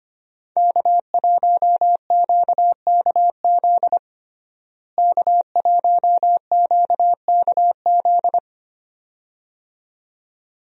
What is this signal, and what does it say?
Morse code, heard as a keyed tone, about 25 wpm, 700 Hz. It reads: K1QKZ K1QKZ
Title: USA Call Signs - 5 Characters - Repeated 2x - 25wpm